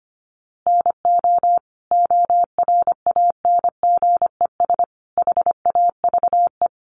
Morse code, heard as a keyed tone, about 25 wpm, 700 Hz. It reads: NO ORANGES HAVE
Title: Easy Phrases 25wpm